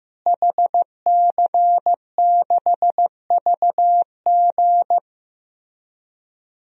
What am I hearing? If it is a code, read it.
Morse code, heard as a keyed tone, about 16 wpm, 700 Hz. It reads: HC6VG
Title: DXCC-100 Call Signs - 5 Characters - Repeated 3x - 15wpm